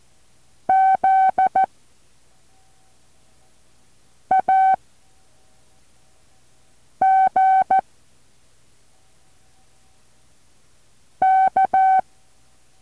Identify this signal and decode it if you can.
Morse code, heard as a keyed tone, about 14 wpm, 760 Hz. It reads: Z A G K
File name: intl morse code2